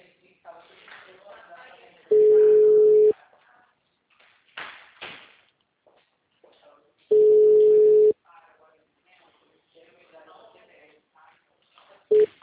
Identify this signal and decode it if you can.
Ringback tone (Europe, 425 Hz), with speech.